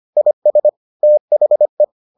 Morse code, heard as a keyed tone, about 25 wpm, 600 Hz. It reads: IS THE